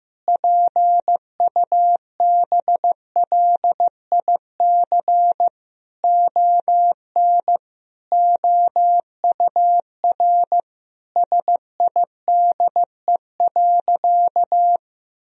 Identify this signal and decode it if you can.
Morse code, heard as a keyed tone, about 15 wpm, 700 Hz. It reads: PUBLIC ON OUR SIDE.
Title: Sentences from Top 1000 Words - Review Part 3 - 15wpm